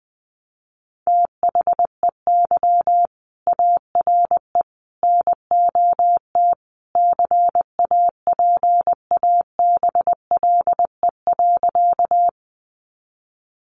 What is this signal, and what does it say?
Morse code, heard as a keyed tone, about 20 wpm, 700 Hz. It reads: THEY ARE NOT CAPABLE.
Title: Sentences from Top 1200 Words 20wpm